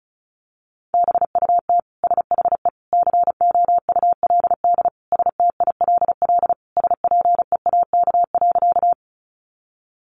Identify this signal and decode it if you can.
Morse code, heard as a keyed tone, about 34 wpm, 700 Hz. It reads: BUT SHE COULD STILL SPEAK.
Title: Sentences from Top 1100 Words 35wpm